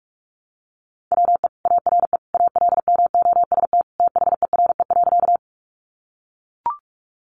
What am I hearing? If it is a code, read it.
Morse code, heard as a keyed tone, about 44 wpm, 700 Hz. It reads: WE ARE ALMOST THERE.
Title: Sentences from Top 700 Words 45wpm